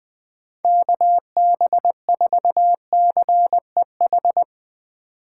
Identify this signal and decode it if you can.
Morse code, heard as a keyed tone, about 20 wpm, 700 Hz. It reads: KB4CEH
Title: USA Call Signs - 6 Characters - 4x Character Spacing - 20wpm